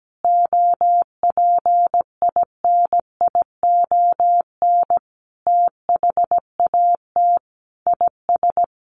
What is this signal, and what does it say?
Morse code, heard as a keyed tone, about 17 wpm, 700 Hz. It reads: OPINION THAT IS
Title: Sentences from Top 1200 Words 17wpm